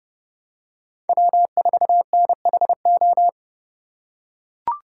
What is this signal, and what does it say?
Morse code, heard as a keyed tone, about 31 wpm, 700 Hz. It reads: W4NHO
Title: USA Call Signs - 5 Characters - 4x Character Spacing - 30wpm